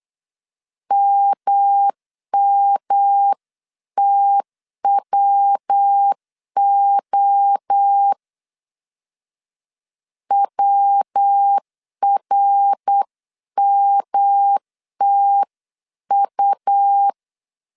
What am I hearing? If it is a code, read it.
Morse code, heard as a keyed tone, about 8 wpm, 800 Hz. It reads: MMTWO WRMTU